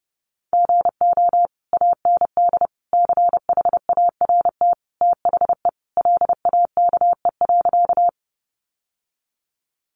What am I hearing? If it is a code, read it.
Morse code, heard as a keyed tone, about 30 wpm, 700 Hz. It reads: GO AND CHART THE LAKE.
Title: Sentences from Top 1300 Words 30wpm